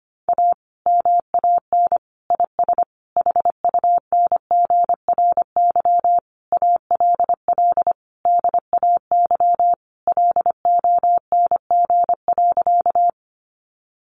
Morse code, heard as a keyed tone, about 25 wpm, 700 Hz. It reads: A MAN IS HUNGRY ALL DAY LONG.
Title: Sentences from Top 1400 Words 25wpm